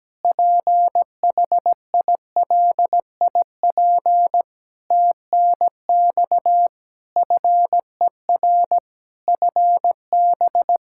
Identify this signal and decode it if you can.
Morse code, heard as a keyed tone, about 17 wpm, 700 Hz. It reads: PHILIP TNX FER FB
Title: K1USN SST QSO Practice 17wpm